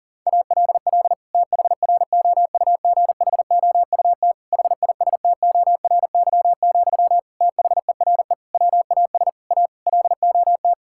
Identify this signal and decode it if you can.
Morse code, heard as a keyed tone, about 40 wpm, 700 Hz. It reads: ALL THROUGHOUT HISTORY, THERE WAS A LOT
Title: Sentences from Top 1000 Words - Review Part 5 - 40wpm